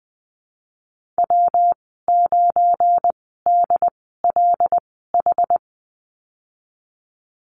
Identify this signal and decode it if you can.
Morse code, heard as a keyed tone, about 19 wpm, spaced as for 14 wpm, 700 Hz. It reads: W9DLH
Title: 173 Lesson 28g - Number 9 - Call Signs 20wpm